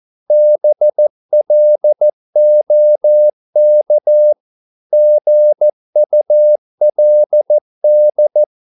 Morse code, heard as a keyed tone, about 14 wpm, 600 Hz. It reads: BLOK GULD
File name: Korte ord DK 14wpm